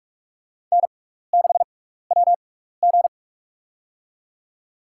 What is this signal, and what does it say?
Morse code, heard as a keyed tone, about 45 wpm, spaced as for 15 wpm, 700 Hz. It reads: N6WG
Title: USA Call Signs - 4 Characters - 6x Character Spacing - 45wpm